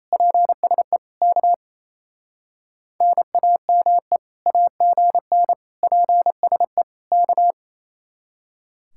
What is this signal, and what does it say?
Morse code, heard as a keyed tone, about 34 wpm, 700 Hz. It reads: PSE K NAME AGN PSE K
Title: Speed-Racing Common QSO Phrases - 28wpm